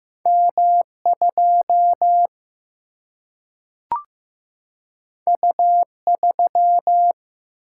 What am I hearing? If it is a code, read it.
Morse code, heard as a keyed tone, about 15 wpm, 700 Hz. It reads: M 2 U 3
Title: Single-Letter Single-Number 15wpm